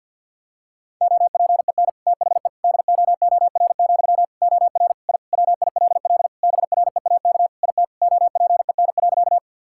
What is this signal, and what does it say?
Morse code, heard as a keyed tone, about 50 wpm, 700 Hz. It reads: OPEN THE DOOR, OR I WILL BREAK IT OPEN.
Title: Sentences from Top 1000 Words - Review Part 6 - 50wpm